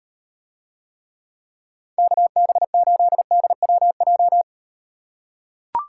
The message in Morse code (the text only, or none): KB8DWJ